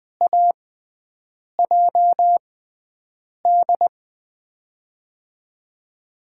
Morse code, heard as a keyed tone, about 20 wpm, spaced as for 7 wpm, 700 Hz. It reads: AJD
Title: USA Call Signs - 6 Characters - 6x Character Spacing - 20wpm